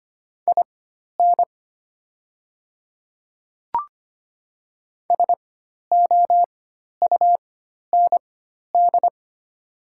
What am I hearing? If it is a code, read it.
Morse code, heard as a keyed tone, about 25 wpm, spaced as for 12 wpm, 700 Hz. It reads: IN SOUND
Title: 4x Character Spacing - Top 300 Words - 25wpm